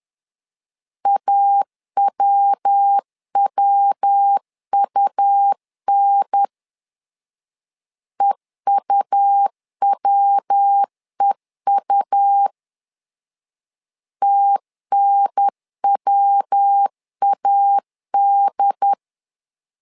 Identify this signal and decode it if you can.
Morse code, heard as a keyed tone, about 10 wpm, 800 Hz. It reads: AWWUN EUWEU TNWAD
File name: LEZIONI / 3 MWRO / 3 MW 10wpm 1959